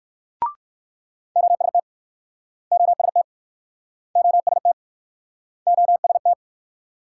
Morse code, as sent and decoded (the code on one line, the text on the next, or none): --- ... - / --- ... - / --- ... - / --- ... -
OST OST OST OST